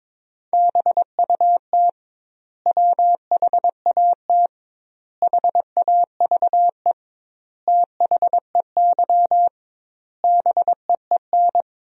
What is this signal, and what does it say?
Morse code, heard as a keyed tone, about 22 wpm, 700 Hz. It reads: BUT WHAT HAVE THEY BEEN
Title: 2x Word-Spacing - Sentences from Top 100 Words - 22wpm